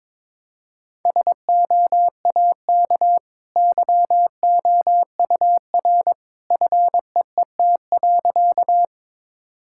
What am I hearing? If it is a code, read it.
Morse code, heard as a keyed tone, about 22 wpm, 690 Hz. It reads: SOAK YOUR FEET.
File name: practice / 22wpm / Easy Sentences with Hints - 22wpm